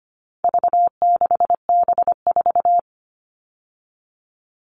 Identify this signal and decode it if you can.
Morse code, heard as a keyed tone, about 25 wpm, 700 Hz. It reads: V6B4
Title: All Cut Numbers - 4-Digit 25wpm